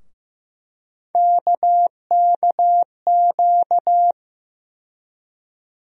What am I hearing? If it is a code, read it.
Morse code, heard as a keyed tone, about 15 wpm, 700 Hz. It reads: KKQ
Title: USA Call Sign Postfixes - 3 Letters 15wpm